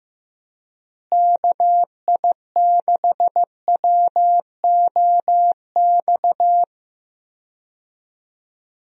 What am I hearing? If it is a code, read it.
Morse code, heard as a keyed tone, about 16 wpm, 700 Hz. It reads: KI6WOX
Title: USA Call Signs - 6 Characters - Repeated 2x - 15wpm